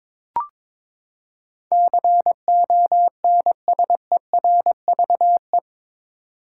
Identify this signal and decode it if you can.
Morse code, heard as a keyed tone, about 22 wpm, 700 Hz. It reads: CONSERVE
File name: practice / 22wpm / Eight Letter Words - 22wpm